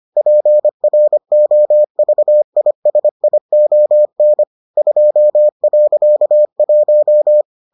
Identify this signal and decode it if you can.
Morse code, heard as a keyed tone, about 25 wpm, 600 Hz. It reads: PROVISION 2.1